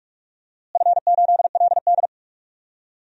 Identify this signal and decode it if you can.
Morse code, heard as a keyed tone, about 46 wpm, 700 Hz. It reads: F8LD
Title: DXCC-100 Call Signs - 4 Characters - Repeated 2x - 45wpm